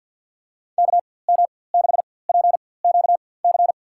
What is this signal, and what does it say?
Morse code, heard as a keyed tone, about 53 wpm, spaced as for 24 wpm, 700 Hz. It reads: KM6PQC
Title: USA Call Signs - 6 Characters - 4x Character Spacing - 50wpm